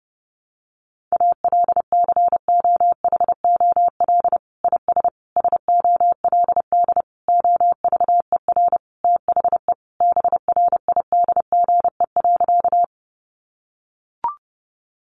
Morse code, heard as a keyed tone, about 30 wpm, 700 Hz. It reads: ALCOHOL IS SOLD OVER THE BRIDGE.
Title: Sentences from Top 1400 Words 30wpm